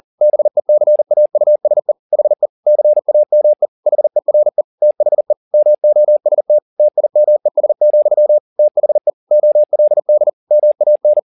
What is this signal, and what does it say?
Morse code, heard as a keyed tone, about 40 wpm, 600 Hz. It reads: BECAUSE HE CAME HERE THE MOST TIMES, THE OLD MAN